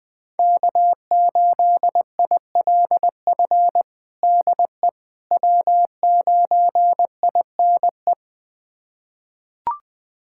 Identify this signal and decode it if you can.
Morse code, heard as a keyed tone, about 20 wpm, 700 Hz. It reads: K8ILF DE W9INE
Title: USA Callsign DE Callsign - 5 Characters - 20wpm